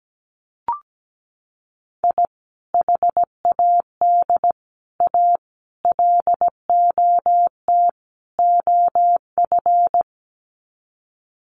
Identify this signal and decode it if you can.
Morse code, heard as a keyed tone, about 17 wpm, 700 Hz. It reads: I HAD A LOT OF
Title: Sets of 5 Words Encore 17wpm